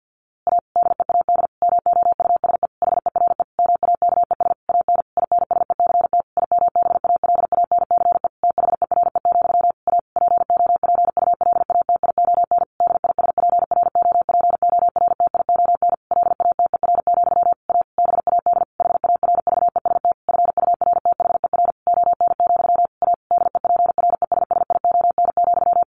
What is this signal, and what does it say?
Morse code, heard as a keyed tone, about 50 wpm, 700 Hz. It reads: A DEAD MOUSE HERE MAKES AN INSECT IMBALANCE THERE, A POPULATION DISPROPORTION LATER, A BAD HARVEST FURTHER ON, A DEPRESSION,
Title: A Sound Of Thunder 50wpm